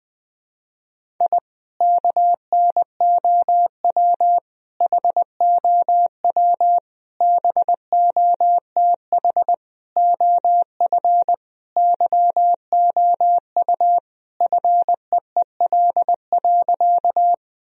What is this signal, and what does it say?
Morse code, heard as a keyed tone, about 21 wpm, 700 Hz. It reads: I KNOW HOW BOTH OF YOU FEEL.
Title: Sentences from Top 1000 Words - Review Part 1 - 20wpm